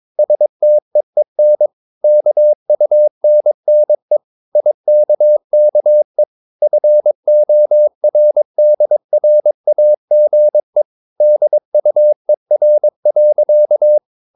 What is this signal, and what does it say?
Morse code, heard as a keyed tone, about 22 wpm, 600 Hz. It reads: STEEN KUNNE IKKE FORDRAGE DUER.
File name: Ferie QRS 22wpm